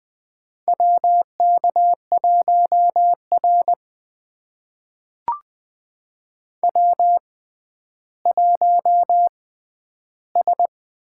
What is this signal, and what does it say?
Morse code, heard as a keyed tone, about 20 wpm, spaced as for 11 wpm, 700 Hz. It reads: WK1R W1S
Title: USA Call Signs - 6x Character Spacing - 20wpm